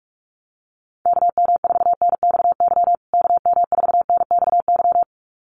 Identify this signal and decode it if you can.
Morse code, heard as a keyed tone, about 44 wpm, 700 Hz. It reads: KM4NXY KM4NXY
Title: USA Call Signs Repeated 2x 45wpm